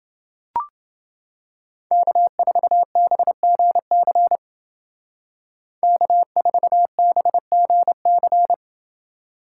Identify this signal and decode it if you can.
Morse code, heard as a keyed tone, about 29 wpm, 700 Hz. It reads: K4BGC K4BGC